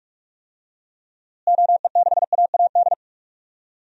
Morse code, heard as a keyed tone, about 46 wpm, 700 Hz. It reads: OE6AAD
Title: DXCC-100 Call Signs - 6 Characters - Repeated 3x - 45wpm